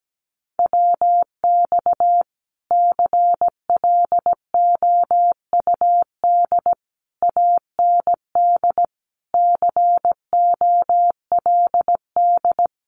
Morse code, heard as a keyed tone, about 17 wpm, 700 Hz. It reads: WX CLOUD AND COLD
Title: Common QSO Phrases Encore 17wpm